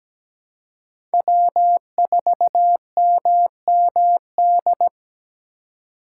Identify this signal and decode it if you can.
Morse code, heard as a keyed tone, about 17 wpm, 700 Hz. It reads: W4MMD
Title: K1USN SST QSO Practice - 17wpm